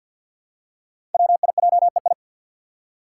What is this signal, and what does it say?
Morse code, heard as a keyed tone, about 47 wpm, 700 Hz. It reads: WIJEI
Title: Five Challenging Letter-Number Combinations 50wpm